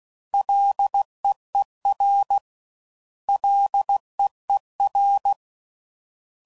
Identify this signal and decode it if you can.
Morse code, heard as a keyed tone, about 16 wpm, 800 Hz. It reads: LEER LEER